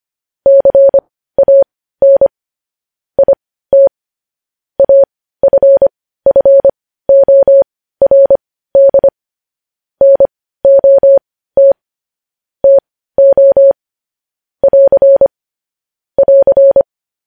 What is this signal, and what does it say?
Morse code, heard as a keyed tone, about 25 wpm, spaced as for 15 wpm, 550 Hz. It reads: CAN IT AFFORD NOT TO <AR> <AR>
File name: podcast / News headlines-1x-25x15